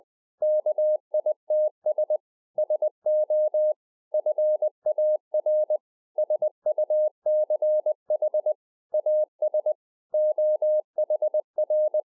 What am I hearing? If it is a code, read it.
Morse code, heard as a keyed tone, about 20 wpm, 610 Hz. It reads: KITS SO FAR SUCH AS OHR
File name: 2011-04-mp3 / A SMALL TRANCEIVER 3 20WPM